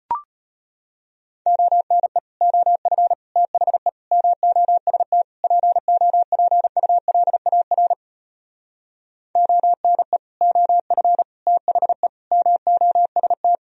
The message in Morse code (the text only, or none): ONE OF THE MOST POPULAR ONE OF THE MOST